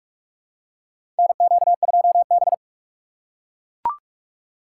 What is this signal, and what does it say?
Morse code, heard as a keyed tone, about 42 wpm, 700 Hz. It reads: NQ2B
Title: USA Call Signs - 4 Characters - 4x Character Spacing - 45wpm